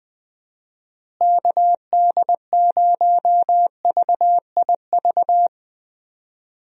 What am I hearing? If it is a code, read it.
Morse code, heard as a keyed tone, about 20 wpm, 700 Hz. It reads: KD0VIV
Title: USA Call Signs - 6 Characters - Repeated 3x - 20wpm